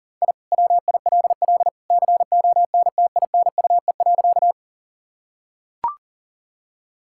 Morse code, heard as a keyed tone, about 40 wpm, 700 Hz. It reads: I WILL CONTINUE.
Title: Sentences from Top 1000 Words - Review Part 6 - 40wpm